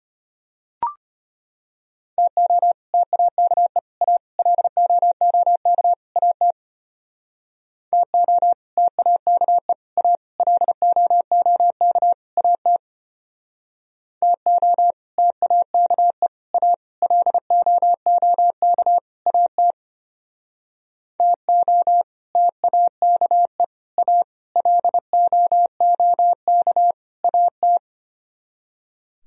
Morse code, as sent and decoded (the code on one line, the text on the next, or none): - --- / - .- -.- . / .- / .-.. --- --- -.- / .- - / - --- / - .- -.- . / .- / .-.. --- --- -.- / .- - / - --- / - .- -.- . / .- / .-.. --- --- -.- / .- - / - --- / - .- -.- . / .- / .-.. --- --- -.- / .- -
TO TAKE A LOOK AT TO TAKE A LOOK AT TO TAKE A LOOK AT TO TAKE A LOOK AT